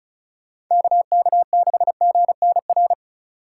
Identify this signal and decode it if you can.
Morse code, heard as a keyed tone, about 43 wpm, 700 Hz. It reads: KK6GNR
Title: Speed Racing - USA 2x3 Call Signs 35wpm